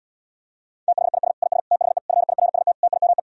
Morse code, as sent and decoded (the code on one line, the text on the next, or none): - .... .. ... / .. ... / - .... / ..-. .. .-. ... - / - .. -- .
THIS IS THE FIRST TIME